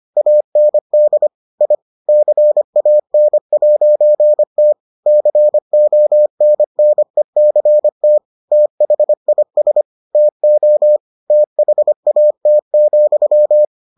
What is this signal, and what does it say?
Morse code, heard as a keyed tone, about 25 wpm, 600 Hz. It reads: AND I CAN'T CONNECT THIS TO THAT,